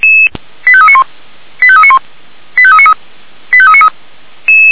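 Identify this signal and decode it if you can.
Five-tone selective call (ZVEI), heard as quick runs of five tones, tones of about 70 ms. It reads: 85391 85391 85393 85393
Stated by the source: ZVEI five-tone selective call 85865 received via SDR, from the feuerwehr-orlamuende fire brigade audio server.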